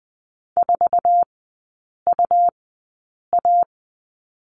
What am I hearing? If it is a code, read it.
Morse code, heard as a keyed tone, about 20 wpm, spaced as for 8 wpm, 690 Hz. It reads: 4UA
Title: 133 Lesson 22e - Question Mark - Three Characters 20wpm